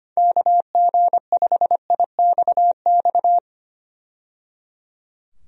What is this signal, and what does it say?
Morse code, heard as a keyed tone, about 25 wpm, 700 Hz. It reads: KG5IXX
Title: USA Call Signs - 6 Characters - 25wpm